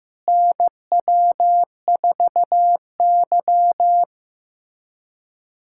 Morse code, heard as a keyed tone, about 15 wpm, 700 Hz. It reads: NW4Y
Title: USA Callsign DE Callsign - 4 Characters - 15wpm